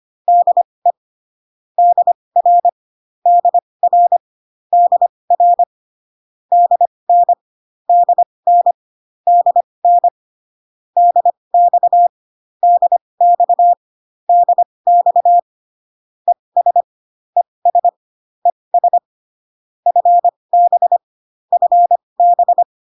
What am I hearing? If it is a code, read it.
Morse code, heard as a keyed tone, about 25 wpm, spaced as for 20 wpm, 700 Hz. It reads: DE DR DR DR DN DN DN DX DX DX ES ES ES FB FB